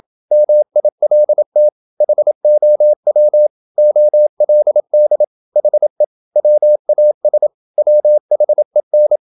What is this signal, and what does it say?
Morse code, heard as a keyed tone, about 27 wpm, 600 Hz. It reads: MILT HOW OLD HE WAS WHEN
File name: Never 0027wpm